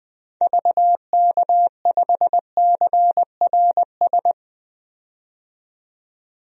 Morse code, heard as a keyed tone, about 20 wpm, 700 Hz. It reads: VK5CRS
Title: DXCC-100 Call Signs - 6 Characters - 20wpm